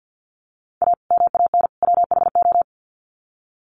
Morse code, heard as a keyed tone, about 50 wpm, 700 Hz. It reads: A MAN WHO
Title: Sets of 3 Words 50wpm